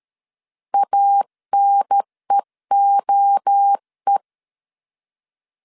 Morse code, heard as a keyed tone, about 13 wpm, 800 Hz. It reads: ANEOE
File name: CW / DET / 1 3 1500C S A 12wpm 2905